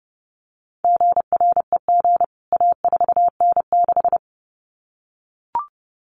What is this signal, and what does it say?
Morse code, heard as a keyed tone, about 30 wpm, 700 Hz. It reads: GREG A4N6
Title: CWT Exchange 30wpm